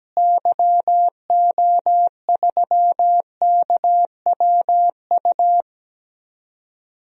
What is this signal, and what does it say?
Morse code, heard as a keyed tone, about 17 wpm, 700 Hz. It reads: YO3KWU
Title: DXCC-100 Call Signs - 6 Characters - Repeated 2x - 17wpm